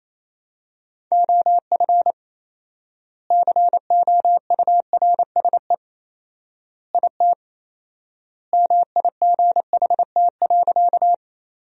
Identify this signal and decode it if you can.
Morse code, heard as a keyed tone, about 28 wpm, 700 Hz. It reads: OF COURSE IT MIGHT.
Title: 4x Word-Spacing - Sentences from Top 500 Words - 28wpm